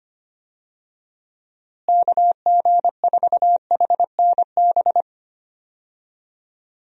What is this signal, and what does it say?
Morse code, heard as a keyed tone, about 25 wpm, 700 Hz. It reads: KG4HNB
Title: USA Call Signs - 6 Characters - Repeated 3x - 25wpm